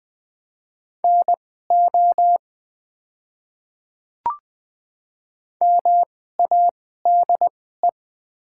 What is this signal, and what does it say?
Morse code, heard as a keyed tone, about 20 wpm, spaced as for 14 wpm, 700 Hz. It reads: NO MADE
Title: 2x Character Spacing - Top 300 Words - 20wpm